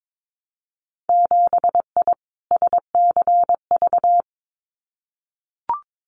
Morse code, heard as a keyed tone, about 22 wpm, 700 Hz. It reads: 7I SCV
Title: Winter Field Day Exchanges Encore 22wpm